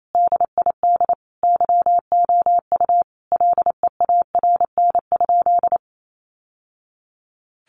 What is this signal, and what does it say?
Morse code, heard as a keyed tone, about 28 wpm, 700 Hz. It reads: DID YOU LEARN?
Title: Sentences from Top 1200 Words 28wpm